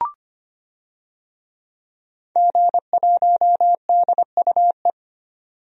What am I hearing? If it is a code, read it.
Morse code, heard as a keyed tone, about 25 wpm, 700 Hz. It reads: G1DUE
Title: DXCC-100 Call Signs - 5 Characters - 25wpm